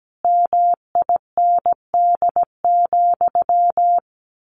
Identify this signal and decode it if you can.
Morse code, heard as a keyed tone, about 17 wpm, 700 Hz. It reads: MIND,
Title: Sentences from Top 1100 Words 17wpm